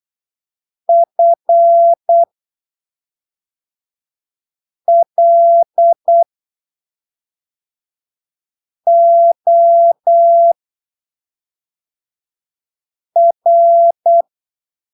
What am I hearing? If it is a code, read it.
Morse code, heard as a keyed tone, about 8 wpm, spaced as for 3 wpm, 680 Hz. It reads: FLOR
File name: arquivos / texto / texto 20 08ppm